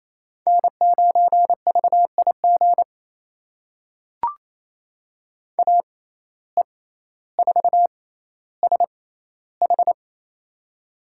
Morse code, heard as a keyed tone, about 28 wpm, spaced as for 14 wpm, 700 Hz. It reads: N9VIG AE4SH
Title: USA Call Signs - 5 Characters - 6x Character Spacing - 28wpm